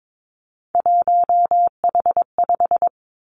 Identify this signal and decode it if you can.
Morse code, heard as a keyed tone, about 22 wpm, 700 Hz. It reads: 1H5
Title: Three Challenging Letter-Number Combinations 22wpm